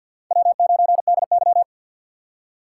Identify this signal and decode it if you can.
Morse code, heard as a keyed tone, about 46 wpm, 700 Hz. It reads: W9DY
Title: K1USN SST QSO Practice - 50wpm